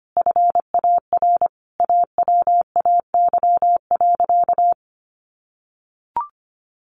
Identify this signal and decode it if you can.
Morse code, heard as a keyed tone, about 25 wpm, 700 Hz. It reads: FAR AWAY.
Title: Sentences from Top 600 Words 25wpm